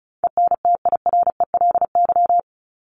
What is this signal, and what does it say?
Morse code, heard as a keyed tone, about 34 wpm, 700 Hz. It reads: ENTIRELY
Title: LY Suffix - Top 100 Words with Base Words 35wpm